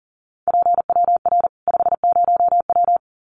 Morse code, heard as a keyed tone, about 40 wpm, 700 Hz. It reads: PWR 50W
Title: Common QSO Phrases 40wpm